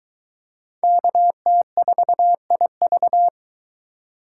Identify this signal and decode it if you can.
Morse code, heard as a keyed tone, about 20 wpm, 700 Hz. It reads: KT4IV